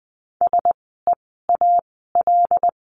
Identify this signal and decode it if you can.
Morse code, heard as a keyed tone, about 20 wpm, spaced as for 14 wpm, 700 Hz. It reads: SEAL
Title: 075 Lesson 13f - Letter L - Words 20wpm